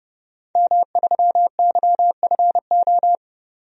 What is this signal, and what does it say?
Morse code, heard as a keyed tone, about 30 wpm, 690 Hz. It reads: M3YFO
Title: DXCC-100 Call Signs - 5 Characters - 30wpm